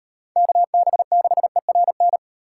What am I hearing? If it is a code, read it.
Morse code, heard as a keyed tone, about 38 wpm, 700 Hz. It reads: KB6ERN